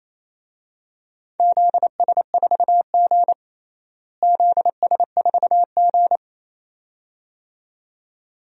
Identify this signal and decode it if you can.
Morse code, heard as a keyed tone, about 28 wpm, 700 Hz. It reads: ZS4G ZS4G
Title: DXCC-100 Call Signs - 4 Characters - Repeated 2x - 28wpm